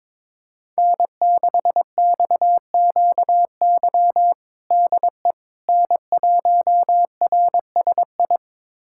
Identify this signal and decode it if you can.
Morse code, heard as a keyed tone, about 21 wpm, 700 Hz. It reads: N6XQY DE N1RSI
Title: USA Callsign DE Callsign - 5 Characters - 22wpm